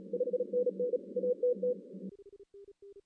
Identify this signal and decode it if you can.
Morse code, heard as a keyed tone, about 32 wpm, 500 Hz. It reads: 5NN ATT
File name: 2016 arrl dx cw / 2016-ARRL-DX-CW-K5TR- - 0-3